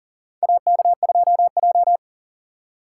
Morse code, heard as a keyed tone, about 39 wpm, 700 Hz. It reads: AK2J
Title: USA Call Signs - 4 Characters - 2x Character Spacing - 40wpm